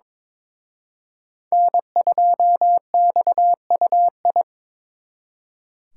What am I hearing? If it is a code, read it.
Morse code, heard as a keyed tone, about 25 wpm, 700 Hz. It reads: N2XUI
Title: Speed Racing - USA 1x3 Call Signs 22wpm